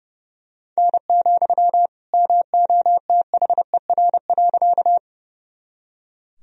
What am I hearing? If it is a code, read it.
Morse code, heard as a keyed tone, about 32 wpm, 700 Hz. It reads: N, MOTHER.
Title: Sentences from Top 200 Words and Abbreviations 30wpm